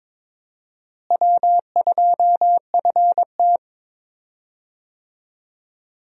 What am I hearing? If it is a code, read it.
Morse code, heard as a keyed tone, about 22 wpm, 700 Hz. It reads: W2FT
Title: USA Call Signs - 4 Characters - Repeated 3x - 22wpm